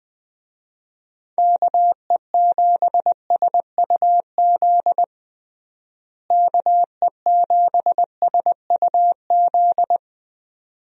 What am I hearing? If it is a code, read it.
Morse code, heard as a keyed tone, about 20 wpm, 700 Hz. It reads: KE7SUZ KE7SUZ
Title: USA Call Signs - 6 Characters - Repeated 2x - 20wpm